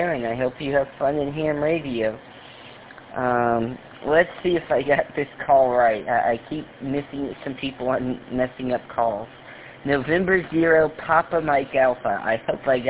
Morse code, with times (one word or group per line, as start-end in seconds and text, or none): none